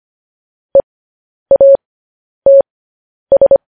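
Morse code, heard as a keyed tone, about 25 wpm, spaced as for 10 wpm, 550 Hz. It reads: EATS
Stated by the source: Morse code transcription of news headlines sent at 10 wpm.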